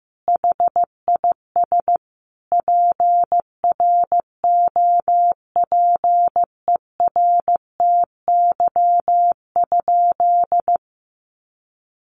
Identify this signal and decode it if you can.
Morse code, heard as a keyed tone, about 14 wpm, 700 Hz. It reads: HIS PROPERTY?
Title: Sentences from Top 1000 Words 15wpm